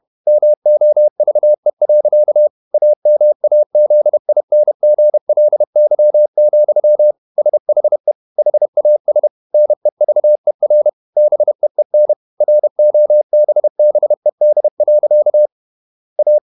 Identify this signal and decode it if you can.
Morse code, heard as a keyed tone, about 31 wpm, 600 Hz. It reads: MOVE. AMAZINGLY, SHE HAS NEVER BEEN ROBBED. A
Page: Never 31wpm | CW med Gnister